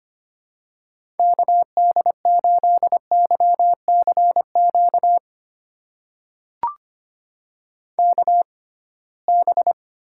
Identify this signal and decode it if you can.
Morse code, heard as a keyed tone, about 25 wpm, spaced as for 19 wpm, 700 Hz. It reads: KD8YCQ KB
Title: USA Call Signs - 6 Characters - 6x Character Spacing - 25wpm